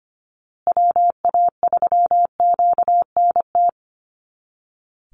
Morse code, heard as a keyed tone, about 25 wpm, 700 Hz. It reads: WA3QNT
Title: Call Signs Repeated 2x - 25wpm